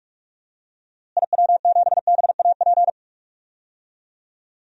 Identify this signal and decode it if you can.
Morse code, heard as a keyed tone, about 46 wpm, 700 Hz. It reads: IW7BAP
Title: DXCC-100 Call Signs - 6 Characters - Repeated 3x - 45wpm